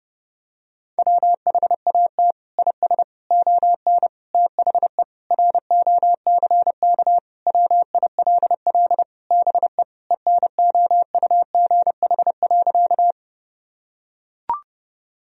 Morse code, heard as a keyed tone, about 30 wpm, 700 Hz. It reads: WHAT IS ON THE ROCK WILL BE ENOUGH.
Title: Sentences from Top 1000 Words - Review Part 2 - 30wpm